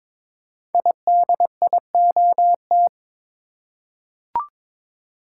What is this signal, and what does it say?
Morse code, heard as a keyed tone, about 22 wpm, 690 Hz. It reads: IDIOT
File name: 5 Letter Words - First 3 Letters Given - 22wpm